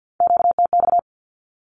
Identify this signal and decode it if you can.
Morse code, heard as a keyed tone, about 50 wpm, 700 Hz. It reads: QTX
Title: Q-Codes 50wpm